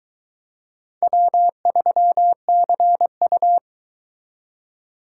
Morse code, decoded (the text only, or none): W3CU